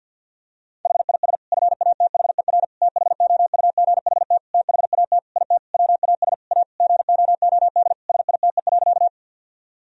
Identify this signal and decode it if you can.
Morse code, heard as a keyed tone, about 50 wpm, 700 Hz. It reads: HIS FATHER THOUGHT THAT IT WAS A GOOD SITE.
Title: Sentences from Top 1000 Words - Review Part 4 - 50wpm